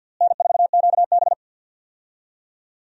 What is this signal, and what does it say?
Morse code, heard as a keyed tone, about 52 wpm, 700 Hz. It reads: N4QB
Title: USA Call Signs - 4 Characters - Repeated 2x - 50wpm